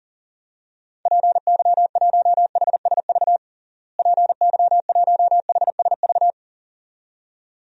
Morse code, heard as a keyed tone, about 40 wpm, 700 Hz. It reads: PY1HSV PY1HSV
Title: DXCC-100 Call Signs - 6 Characters - Repeated 2x - 40wpm